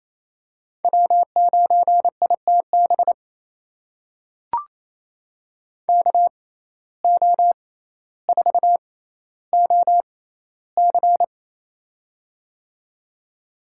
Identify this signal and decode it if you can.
Morse code, heard as a keyed tone, about 28 wpm, spaced as for 14 wpm, 700 Hz. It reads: W9ITB KO4OC
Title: USA Call Signs - 6x Character Spacing - 28wpm